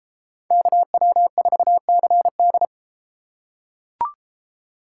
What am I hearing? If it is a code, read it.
Morse code, heard as a keyed tone, about 32 wpm, 700 Hz. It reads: KW4CD